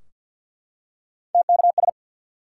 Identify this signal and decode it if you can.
Morse code, heard as a keyed tone, about 50 wpm, 700 Hz. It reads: TKS
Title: Top 90 QSO Elements 50wpm